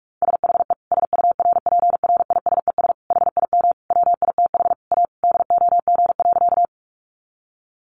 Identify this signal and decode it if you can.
Morse code, heard as a keyed tone, about 44 wpm, 700 Hz. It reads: SHE SURPRISES HIM WITH A DOG.
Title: Additional Easy Phrases 45wpm